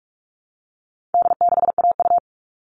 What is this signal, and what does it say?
Morse code, heard as a keyed tone, about 44 wpm, 700 Hz. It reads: D6AU
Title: All Cut Numbers - 4-Digit 45wpm